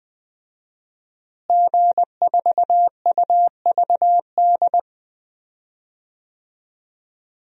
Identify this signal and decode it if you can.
Morse code, heard as a keyed tone, about 20 wpm, 700 Hz. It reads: G4UVD
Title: DXCC-100 Call Signs - 5 Characters - Repeated 2x - 20wpm